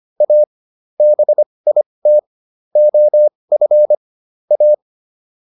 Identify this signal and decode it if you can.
Morse code, heard as a keyed tone, about 25 wpm, spaced as for 20 wpm, 600 Hz. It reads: A BIT OF A